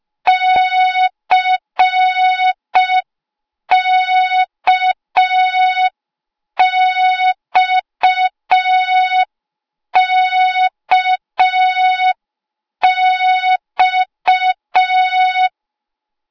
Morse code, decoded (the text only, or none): CKXKX